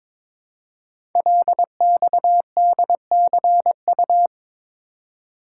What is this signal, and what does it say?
Morse code, heard as a keyed tone, about 22 wpm, 700 Hz. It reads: LXDCU
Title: Code Groups - 5 Letters - 22wpm